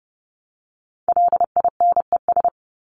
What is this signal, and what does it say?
Morse code, heard as a keyed tone, about 30 wpm, 700 Hz. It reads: LINES
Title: ES:S Suffix - Top 100 Words with Base Words 30wpm